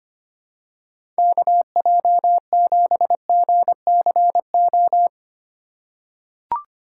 Morse code, heard as a keyed tone, about 25 wpm, 700 Hz. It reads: KJ7GCO
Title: USA Call Signs - 3x Character Spacing - 25wpm